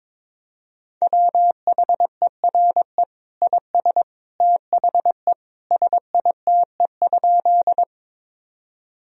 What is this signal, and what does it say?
Morse code, heard as a keyed tone, about 22 wpm, 700 Hz. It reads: WHERE IS THE SITE?
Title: Sentences from Top 1000 Words - Review Part 4 - 22wpm